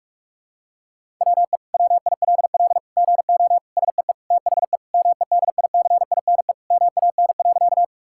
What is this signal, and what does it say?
Morse code, heard as a keyed tone, about 46 wpm, 700 Hz. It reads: WE WILL GO SEE THE MEDICINE MAN.
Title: Sentences from Top 1000 Words - Review Part 6 - 45wpm